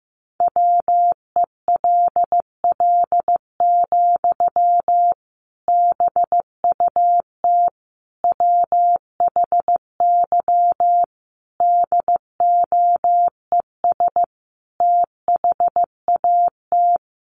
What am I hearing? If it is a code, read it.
Morse code, heard as a keyed tone, about 14 wpm, 700 Hz. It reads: WELL, BUT WHY DOES THAT
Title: Sentences from Top 700 Words 15wpm